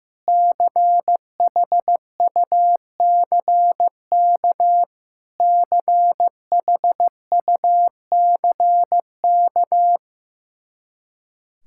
Morse code, heard as a keyed tone, about 16 wpm, 700 Hz. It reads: CHUCK CHUCK
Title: K1USN SST QSO Practice 15wpm